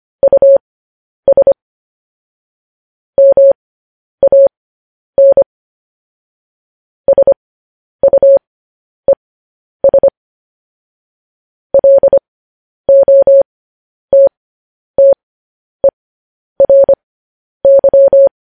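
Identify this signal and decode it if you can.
Morse code, heard as a keyed tone, about 25 wpm, spaced as for 10 wpm, 550 Hz. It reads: US MAN SUES LOTTERY
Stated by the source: Morse code news headlines practice transmission.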